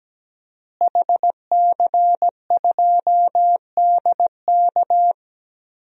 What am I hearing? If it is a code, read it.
Morse code, heard as a keyed tone, about 17 wpm, 700 Hz. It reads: HC2DK
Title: DXCC-100 Call Signs - 5 Characters - Repeated 3x - 17wpm